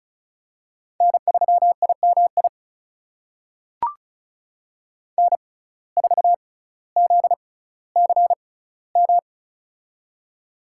Morse code, heard as a keyed tone, about 33 wpm, spaced as for 18 wpm, 700 Hz. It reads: N3IMI N4ZCM
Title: USA Call Signs - 5 Characters - 6x Character Spacing - 35wpm